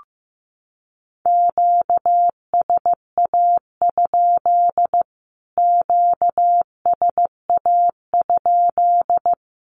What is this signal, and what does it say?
Morse code, heard as a keyed tone, about 15 wpm, 700 Hz. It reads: QSA? QSA?
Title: Q-Codes Repeated 3x 15wpm